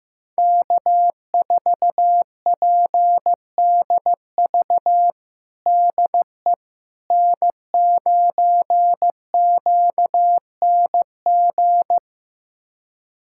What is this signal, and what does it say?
Morse code, heard as a keyed tone, about 15 wpm, 700 Hz. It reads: K4PDV DE N9QNG
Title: USA Callsign DE Callsign 15wpm